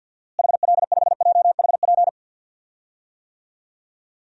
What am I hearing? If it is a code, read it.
Morse code, heard as a keyed tone, about 48 wpm, 700 Hz. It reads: HL5JHP
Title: DXCC-100 Call Signs - 6 Characters - Repeated 2x - 50wpm